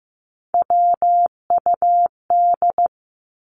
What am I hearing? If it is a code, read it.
Morse code, heard as a keyed tone, about 15 wpm, 700 Hz. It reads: WUD
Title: Abbreviations 15wpm